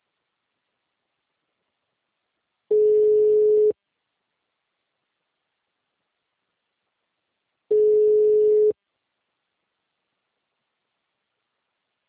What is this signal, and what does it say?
Ringback tone (Europe, 425 Hz).
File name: out- -2051- - - .7542